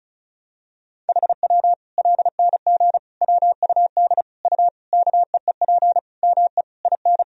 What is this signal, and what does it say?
Morse code, heard as a keyed tone, about 34 wpm, 700 Hz. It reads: HW LNG WUD U KEEP ME IN
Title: Sentences from Top 500 Words and Abbreviations 35wpm